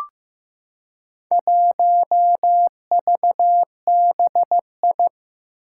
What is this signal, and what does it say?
Morse code, heard as a keyed tone, about 15 wpm, 700 Hz. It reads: 1VBI
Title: Four Challenging Letter-Number Combinations 15wpm